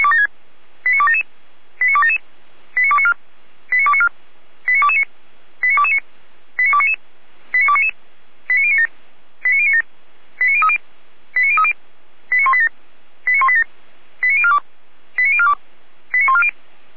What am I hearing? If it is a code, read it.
Five-tone selective call (ZVEI), heard as quick runs of five tones, tones of about 80 ms. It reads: 89277 89277 89294 89294 89229 89229 89299 89299 89097 89097 89030 89030 89187 89187 89042 89042 89140
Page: /455/SDR/ZVEI/89310/2022.10.05/